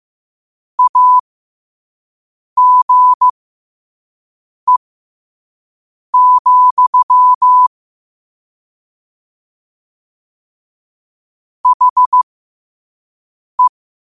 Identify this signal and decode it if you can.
Morse code, heard as a keyed tone, about 15 wpm, spaced as for 5 wpm, 1000 Hz. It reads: AGE, HE
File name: qotdfiles / 5wpm